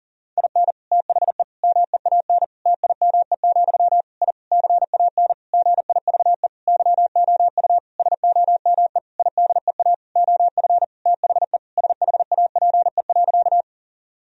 Morse code, heard as a keyed tone, about 40 wpm, 700 Hz. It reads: IN THE MEAN TIME, I CAN GIVE YOU SOME IDEA OF THE SHAPE.
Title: Sentences from Top 1000 Words - Review Part 3 - 40wpm